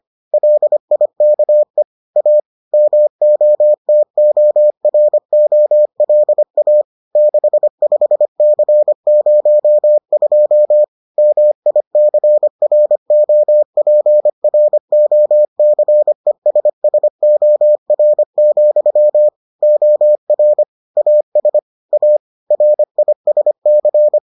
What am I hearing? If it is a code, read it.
Morse code, heard as a keyed tone, about 25 wpm, 600 Hz. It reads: LIKE A MOTOROLA 65C02 MICROPROCESSOR, OR AS A RISC